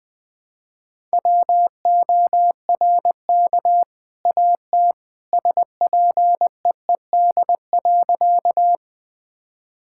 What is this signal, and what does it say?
Morse code, heard as a keyed tone, about 21 wpm, 700 Hz. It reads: WORK AT SPEED.
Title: Sentences from Top 1000 Words - Review Part 5 - 20wpm